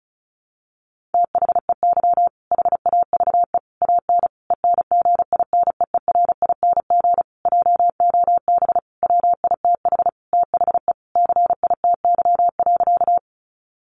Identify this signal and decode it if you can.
Morse code, heard as a keyed tone, about 34 wpm, 700 Hz. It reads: THEY HAVE AN ENGINEERING JOB WITH THE CITY.
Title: Sentences from Top 1500 Words 35wpm